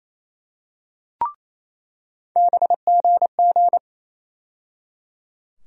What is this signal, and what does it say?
Morse code, heard as a keyed tone, about 28 wpm, 700 Hz. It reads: BGG
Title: USA Call Sign Postfixes - 3 Letters 28wpm